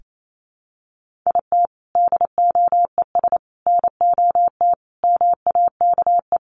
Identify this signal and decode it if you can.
Morse code, heard as a keyed tone, about 28 wpm, 700 Hz. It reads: IT DOES NOT MAKE
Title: Sets of 5 Words Encore2 28wpm